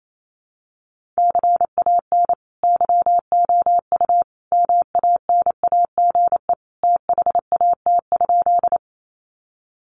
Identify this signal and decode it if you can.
Morse code, heard as a keyed tone, about 28 wpm, 700 Hz. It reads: CAN YOU MANAGE THAT?
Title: Sentences from Top 1400 Words 28wpm